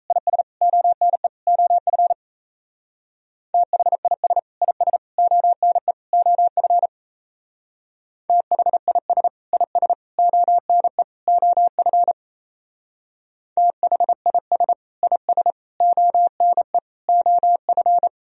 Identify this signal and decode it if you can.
Morse code, heard as a keyed tone, about 35 wpm, 700 Hz. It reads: IS ONE OF THIS IS ONE OF THIS IS ONE OF THIS IS ONE OF
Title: Speed-Racing - Sets of 4 Words 28wpm